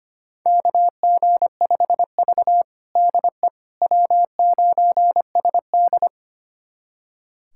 Morse code, heard as a keyed tone, about 25 wpm, 700 Hz. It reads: KG5V DE W9SD
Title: USA Callsign DE Callsign - 4 Characters - 25wpm